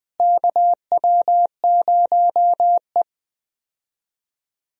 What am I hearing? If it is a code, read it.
Morse code, heard as a keyed tone, about 20 wpm, 700 Hz. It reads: KW0E
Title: USA Call Signs - 4 Characters - Repeated 3x - 20wpm